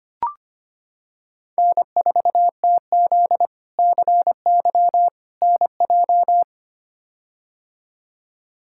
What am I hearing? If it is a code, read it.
Morse code, heard as a keyed tone, about 25 wpm, 700 Hz. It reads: N4TZ CY NJ
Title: K1USN SST QSO Practice - 25wpm